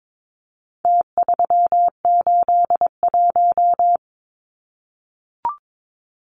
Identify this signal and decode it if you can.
Morse code, heard as a keyed tone, about 22 wpm, 700 Hz. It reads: T381
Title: Numbers - Contest Style - 4-Digit 22wpm